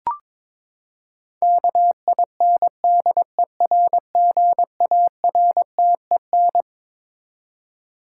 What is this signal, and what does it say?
Morse code, heard as a keyed tone, about 22 wpm, 700 Hz. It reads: KINDERGARTEN
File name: Twelve Letter Words - 22wpm